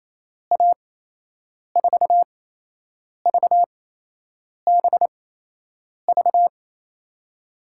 Morse code, heard as a keyed tone, about 28 wpm, spaced as for 8 wpm, 700 Hz. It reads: A4VBV
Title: USA Call Signs - 6 Characters - 8x Character Spacing - 28wpm